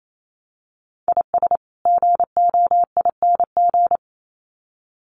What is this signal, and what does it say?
Morse code, heard as a keyed tone, about 28 wpm, 700 Hz. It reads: IS GOING
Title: Sets of 2 Words 28wpm